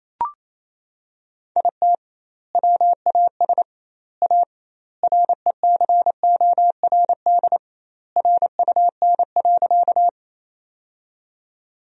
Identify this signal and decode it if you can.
Morse code, heard as a keyed tone, about 28 wpm, 700 Hz. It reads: IT WAS A RECORD RUN.
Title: 2x Word-Spacing - Sentences from Top 500 Words - 28wpm